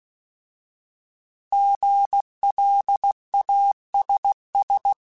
Morse code, heard as a keyed tone, about 16 wpm, 800 Hz. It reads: GLASS
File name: Alphabetical / Alphabetical mp3s / Gs @15